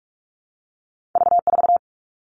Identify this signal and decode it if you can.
Morse code, heard as a keyed tone, about 46 wpm, 700 Hz. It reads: V4
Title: All Cut Numbers - 2-Digit 45wpm